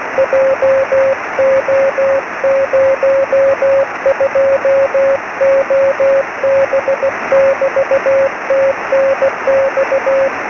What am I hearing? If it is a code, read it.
Morse code, heard as a keyed tone, about 16 wpm, 550 Hz, with noise. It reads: JO02OB <BT> TNX
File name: sm7lcb / g3xdy 23cm